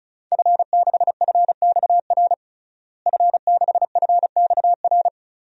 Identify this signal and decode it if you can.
Morse code, heard as a keyed tone, about 36 wpm, 700 Hz. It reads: F6FXR F6FXR
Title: DXCC-100 Call Signs - 5 Characters - Repeated 2x - 35wpm